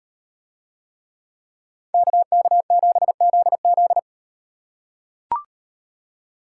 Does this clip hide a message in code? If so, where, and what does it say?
Yes: Morse code from 1.9 to 4.0 s, heard as a keyed tone, about 38 wpm, 700 Hz. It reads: KK7ZZ